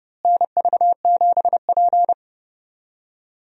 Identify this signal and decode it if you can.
Morse code, heard as a keyed tone, about 30 wpm, 700 Hz. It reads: NV7P
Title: USA Call Signs - 4 Characters - Repeated 3x - 30wpm